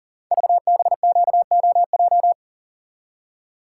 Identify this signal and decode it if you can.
Morse code, heard as a keyed tone, about 40 wpm, 690 Hz. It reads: VBQOJ
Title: Code Groups - 5 Letters - 40wpm